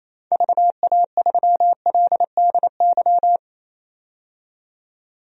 Morse code, heard as a keyed tone, about 28 wpm, 700 Hz. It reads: VA3LDY
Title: DXCC-100 Call Signs - 6 Characters - 28wpm